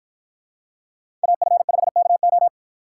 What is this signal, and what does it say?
Morse code, heard as a keyed tone, about 53 wpm, 700 Hz. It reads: AF5KO